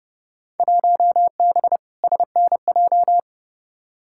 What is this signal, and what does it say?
Morse code, heard as a keyed tone, about 30 wpm, 700 Hz. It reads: 1B SNJ
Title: Field Day Exchanges Encore - 30wpm